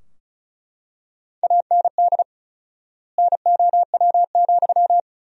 Morse code, heard as a keyed tone, about 36 wpm, 700 Hz. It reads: AND NOW,
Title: 4x Word-Spacing - Sentences from Top 300 Words - 35wpm